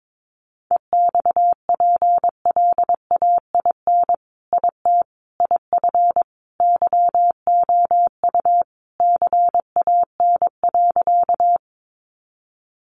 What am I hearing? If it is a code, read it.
Morse code, heard as a keyed tone, about 22 wpm, 700 Hz. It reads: EXPLAIN IT IF YOU CAN.
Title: Sentences from Top 1100 Words 22wpm